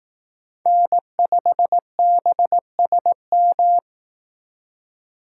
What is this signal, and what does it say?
Morse code, heard as a keyed tone, about 18 wpm, 700 Hz. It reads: N5BSM
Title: Speed Racing - USA 1x3 Call Signs 15wpm